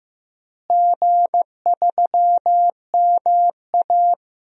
Morse code, heard as a keyed tone, about 15 wpm, 700 Hz. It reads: G3MA
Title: DXCC-100 Call Signs - 4 Characters - Repeated 3x - 15wpm